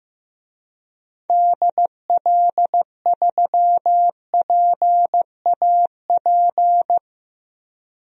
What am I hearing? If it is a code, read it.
Morse code, heard as a keyed tone, about 15 wpm, 700 Hz. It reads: DL3PAP
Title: DXCC-100 Call Signs - 6 Characters - Repeated 2x - 15wpm